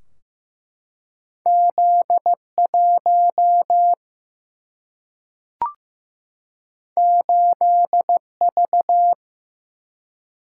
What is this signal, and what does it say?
Morse code, heard as a keyed tone, about 15 wpm, 700 Hz. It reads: Z1 8V
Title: Single-Letter Single-Number 15wpm